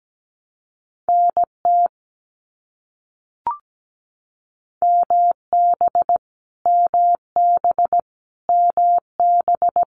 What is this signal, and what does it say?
Morse code, heard as a keyed tone, about 17 wpm, 700 Hz. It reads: NT MB MB MB
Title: Canadian Provinces and Territories - Repeated 3x - 17wpm